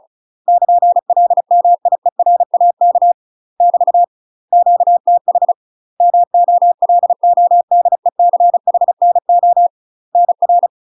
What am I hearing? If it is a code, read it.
Morse code, heard as a keyed tone, about 36 wpm, 700 Hz. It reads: <KN> LMIERAK <BT> QTH MOLODECHNO NR
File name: UK MASTER QSOs TRAINING / 35WPM / UK MASTER QSOs 0014